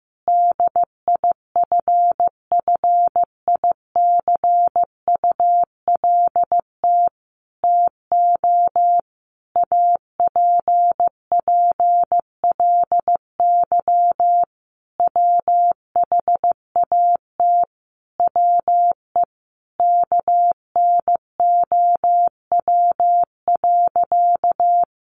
Morse code, heard as a keyed tone, about 15 wpm, 700 Hz. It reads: DIFFICULT TO APPLY WHAT WE KNOW.
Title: Sentences from Top 1200 Words 15wpm